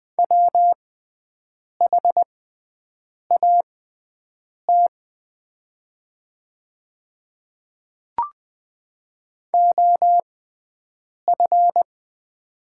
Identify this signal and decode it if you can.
Morse code, heard as a keyed tone, about 20 wpm, spaced as for 7 wpm, 700 Hz. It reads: WHAT OF
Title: 6x Character Spacing - Top 300 Words - 20wpm